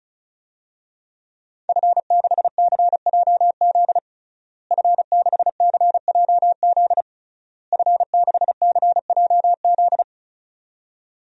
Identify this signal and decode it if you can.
Morse code, heard as a keyed tone, about 36 wpm, 700 Hz. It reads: F6CJZ F6CJZ F6CJZ
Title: DXCC-100 Call Signs - 5 Characters - Repeated 3x - 35wpm